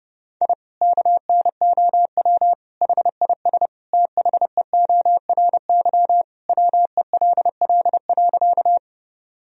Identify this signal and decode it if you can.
Morse code, heard as a keyed tone, about 30 wpm, 700 Hz. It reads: I KNOW HIS THEORY WELL.
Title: Sentences from Top 1000 Words - Review Part 3 - 30wpm